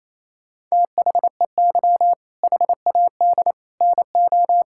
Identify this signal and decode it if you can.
Morse code, heard as a keyed tone, about 28 wpm, 700 Hz. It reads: THEY HAD NO
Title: Sentences from Top 1000 Words - Review Part 6 - 28wpm